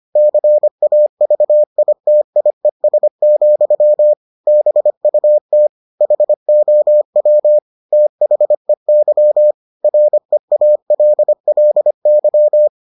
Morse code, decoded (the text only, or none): CAVITIES, BUT HOW THEY REALLY